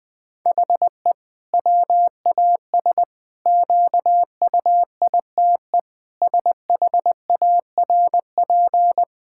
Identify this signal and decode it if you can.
Morse code, heard as a keyed tone, about 20 wpm, 700 Hz. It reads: HE WAS QUITE SHARP
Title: Sentences from Top 1600 Words 20wpm